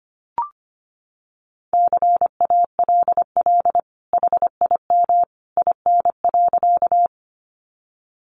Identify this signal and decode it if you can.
Morse code, heard as a keyed tone, about 25 wpm, 700 Hz. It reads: CALL HIM IN.
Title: Sentences from Top 100 Words 25wpm